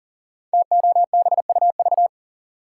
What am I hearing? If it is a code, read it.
Morse code, heard as a keyed tone, about 42 wpm, 700 Hz. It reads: TOBUV
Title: Code Groups - 5 Letters - 40wpm